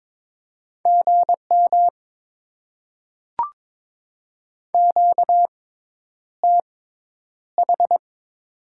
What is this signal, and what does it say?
Morse code, heard as a keyed tone, about 22 wpm, spaced as for 10 wpm, 700 Hz. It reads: GM QTH
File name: Top 20 QSO Elements - 6x Character Spacing - 22wpm